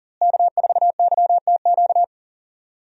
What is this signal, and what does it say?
Morse code, heard as a keyed tone, about 40 wpm, 700 Hz. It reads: K4YTQ
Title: USA Call Signs - 5 Characters - Repeated 3x - 40wpm